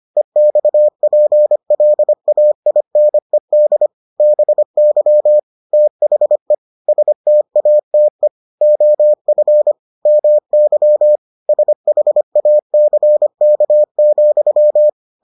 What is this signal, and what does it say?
Morse code, heard as a keyed tone, about 25 wpm, 600 Hz. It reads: EXPLAINED BY THE STATE OF MY SHACK,